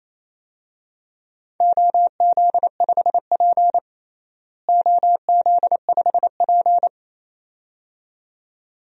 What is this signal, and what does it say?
Morse code, heard as a keyed tone, about 27 wpm, 700 Hz. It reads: OZ5P OZ5P
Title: DXCC-100 Call Signs - 4 Characters - Repeated 2x - 28wpm